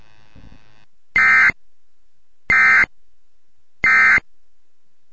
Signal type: Emergency Alert System (SAME) tones